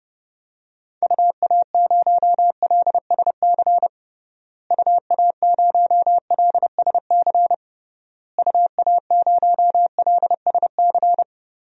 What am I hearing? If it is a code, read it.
Morse code, heard as a keyed tone, about 32 wpm, 700 Hz. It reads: UA0LSC UA0LSC UA0LSC
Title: DXCC-100 Call Signs - 6 Characters - Repeated 3x - 30wpm